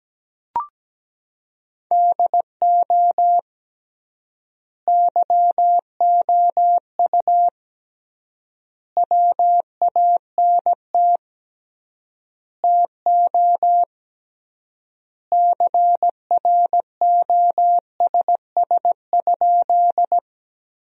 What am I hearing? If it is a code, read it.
Morse code, heard as a keyed tone, about 17 wpm, 700 Hz. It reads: DO YOU WANT TO CROSS?
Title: 3x Word-Spacing - Sentences from Top 300 Words - 17wpm